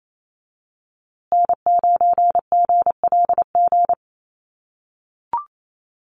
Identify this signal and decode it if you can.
Morse code, heard as a keyed tone, about 28 wpm, 700 Hz. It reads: N9GLG
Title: USA Call Signs Repeated 2x 28wpm